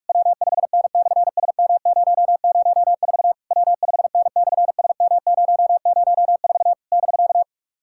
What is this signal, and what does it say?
Morse code, heard as a keyed tone, about 48 wpm, 700 Hz. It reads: W5N/SM004 W5N/SM004 <BK>
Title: SOTA QSO Practice 45wpm